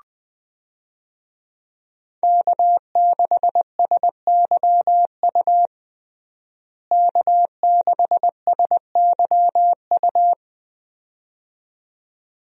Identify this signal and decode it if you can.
Morse code, heard as a keyed tone, about 20 wpm, 700 Hz. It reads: K6SYU K6SYU
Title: USA Call Signs - 5 Characters - Repeated 2x - 20wpm